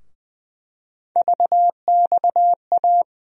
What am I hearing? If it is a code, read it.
Morse code, heard as a keyed tone, about 20 wpm, 700 Hz. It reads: VXA
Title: USA Call Sign Postfixes - 3 Letters 20wpm